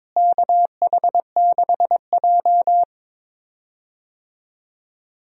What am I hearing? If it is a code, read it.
Morse code, heard as a keyed tone, about 22 wpm, 700 Hz. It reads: KH6J
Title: DXCC-100 Call Signs - 4 Characters - Repeated 2x - 22wpm